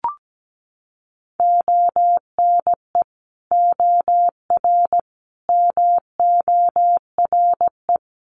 Morse code, heard as a keyed tone, about 17 wpm, 700 Hz. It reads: ONE OR MORE
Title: Sets of 5 Words Encore 17wpm